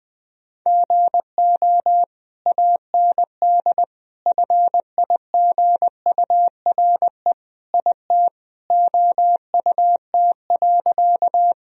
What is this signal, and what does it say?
Morse code, heard as a keyed tone, about 20 wpm, 700 Hz. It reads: GO AND FIGURE IT OUT.
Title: Sentences from Top 1000 Words - Review Part 2 - 20wpm